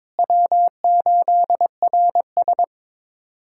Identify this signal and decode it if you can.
Morse code, heard as a keyed tone, about 22 wpm, 700 Hz. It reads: W8RS
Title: USA Callsign DE Callsign 22wpm